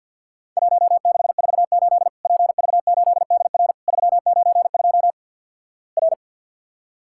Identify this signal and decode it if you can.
Morse code, heard as a keyed tone, about 50 wpm, 700 Hz. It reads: 1648 PV8DR 392 R
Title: Worked All Europe DX Contest - QTC Exchange - 50wpm